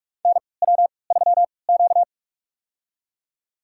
Morse code, heard as a keyed tone, about 44 wpm, spaced as for 25 wpm, 690 Hz. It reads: NW3Q
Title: USA Call Signs - 4 Characters - 3x Character Spacing - 45wpm